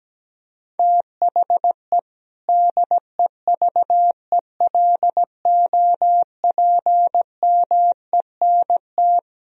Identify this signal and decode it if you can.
Morse code, heard as a keyed tone, about 17 wpm, 700 Hz. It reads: THE DEVELOPMENT
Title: Speed-Racing - Sets of 3 Words 17wpm